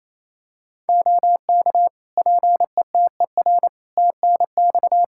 Morse code, heard as a keyed tone, about 28 wpm, 700 Hz. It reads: OK PETER TNX
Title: K1USN SST QSO Practice 28wpm